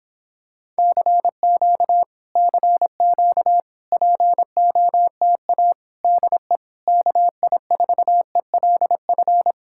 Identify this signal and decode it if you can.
Morse code, heard as a keyed tone, about 26 wpm, 700 Hz. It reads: CQ CQ POTA DE KI4ELF